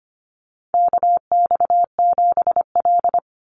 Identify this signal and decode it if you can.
Morse code, heard as a keyed tone, about 25 wpm, 700 Hz. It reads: KX7L
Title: NA SOTA Chaser Callsigns 25wpm